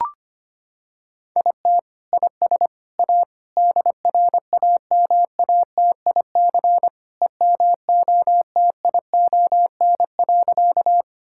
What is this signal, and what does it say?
Morse code, heard as a keyed tone, about 25 wpm, 700 Hz. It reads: IT IS A DRAMATIC EMOTION.
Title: Sentences from Top 1600 Words 25wpm